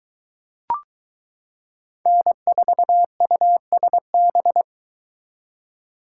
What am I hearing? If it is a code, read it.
Morse code, heard as a keyed tone, about 22 wpm, 700 Hz. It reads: N4USB